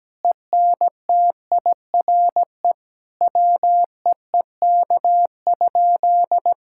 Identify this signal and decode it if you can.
Morse code, heard as a keyed tone, about 17 wpm, 700 Hz. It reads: ENTIRE WEEK?
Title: Sentences from Top 1000 Words - Review Part 5 - 17wpm